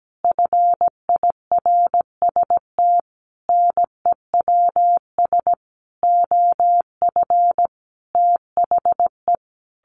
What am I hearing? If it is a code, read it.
Morse code, heard as a keyed tone, about 17 wpm, 700 Hz. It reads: FIRST NEWS OF THE
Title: Sentences from Top 1500 Words 17wpm